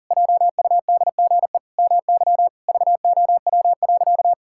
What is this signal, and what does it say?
Morse code, heard as a keyed tone, about 40 wpm, 700 Hz. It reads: JUDGE MY VOW.
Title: Pangram Challenge v2 - 40wpm